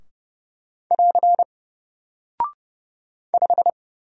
Morse code, heard as a keyed tone, about 31 wpm, 700 Hz. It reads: <AR> 5
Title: Single Character (Letter:Number:Punctuation:Prosign) Alt 30wpm